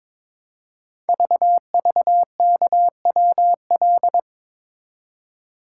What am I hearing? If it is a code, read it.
Morse code, heard as a keyed tone, about 22 wpm, 700 Hz. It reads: VVKWL
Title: Code Groups - 5 Letters - 22wpm